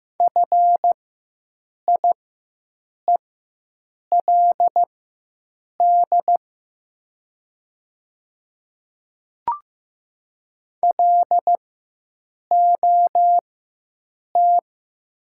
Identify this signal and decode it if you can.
Morse code, heard as a keyed tone, about 15 wpm, spaced as for 7 wpm, 700 Hz. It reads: FIELD LOT
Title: 4x Character Spacing - Top 500 Words - 15wpm